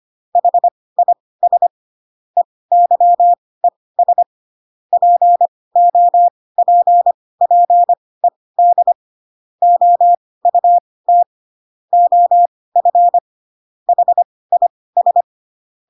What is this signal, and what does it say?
Morse code, heard as a keyed tone, about 25 wpm, spaced as for 18 wpm, 700 Hz. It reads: HIS EYES POPPED OUT OF HIS
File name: Pinocchio / Pinocchio ch1 12 25 ik1hge